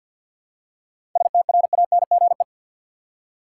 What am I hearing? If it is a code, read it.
Morse code, heard as a keyed tone, about 48 wpm, 700 Hz. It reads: STRANGE
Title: Seven Letter Words - 50wpm